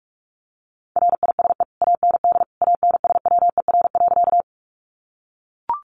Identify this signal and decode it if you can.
Morse code, heard as a keyed tone, about 44 wpm, 700 Hz. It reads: RISE AND ANSWER.
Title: Sentences from Top 900 Words 45wpm